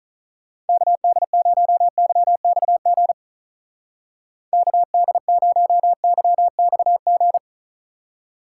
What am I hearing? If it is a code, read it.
Morse code, heard as a keyed tone, about 42 wpm, 700 Hz. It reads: KD0YXG KD0YXG
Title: Speed Racing - USA 2x3 Call Signs 35wpm